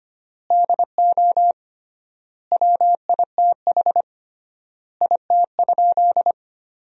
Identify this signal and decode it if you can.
Morse code, heard as a keyed tone, about 25 wpm, 700 Hz. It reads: DO WITH IT?
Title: 3x Word-Spacing - Sentences from Top 100 Words - 25wpm